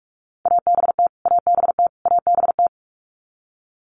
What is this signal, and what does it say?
Morse code, heard as a keyed tone, about 44 wpm, 700 Hz. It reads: ABT ABT ABT
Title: Abbreviations Repeated 3x - 45wpm